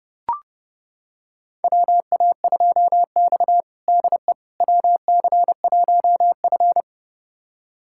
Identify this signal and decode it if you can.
Morse code, heard as a keyed tone, about 32 wpm, 700 Hz. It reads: WA2X DE WC1F
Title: USA Callsign DE Callsign - 4 Characters - 30wpm